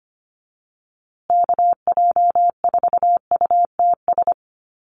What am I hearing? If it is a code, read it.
Morse code, heard as a keyed tone, about 25 wpm, 700 Hz. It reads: KJ4UTS
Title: USA Call Signs Repeated 3x 25wpm